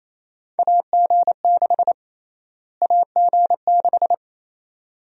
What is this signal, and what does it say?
Morse code, heard as a keyed tone, about 28 wpm, 700 Hz. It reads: AG6 AG6
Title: USA Call Sign Prefixes - Repeated 2x - 28wpm